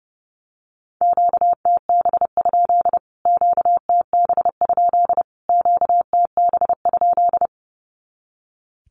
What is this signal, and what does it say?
Morse code, heard as a keyed tone, about 30 wpm, 700 Hz. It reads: QTB? QTB? QTB?
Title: Q-Codes Repeated 3x 30wpm